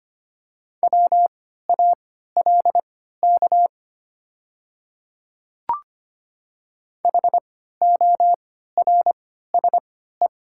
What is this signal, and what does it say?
Morse code, heard as a keyed tone, about 25 wpm, spaced as for 14 wpm, 700 Hz. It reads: WALK HORSE
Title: 3x Character Spacing - Top 500 Words - 25wpm